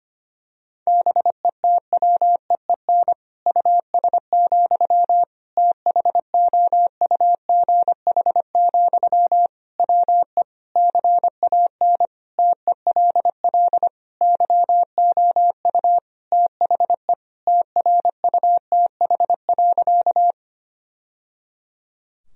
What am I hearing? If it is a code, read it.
Morse code, heard as a keyed tone, about 25 wpm, 700 Hz. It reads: BETWEEN US, THOUGH, WE CAN TELL YOU THE TRUTH.
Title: Sentences from Top 1000 Words - Review Part 2 - 25wpm